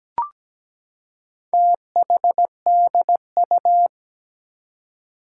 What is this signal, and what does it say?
Morse code, heard as a keyed tone, about 17 wpm, 700 Hz. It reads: THDU
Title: Four Challenging Letter-Number Combinations 17wpm